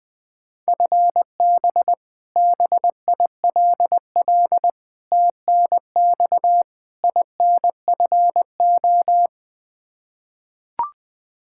Morse code, heard as a keyed tone, about 21 wpm, 700 Hz. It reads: FB BILL TNX INFO
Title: Practice QSO - Exchange 3 - More Info - 20wpm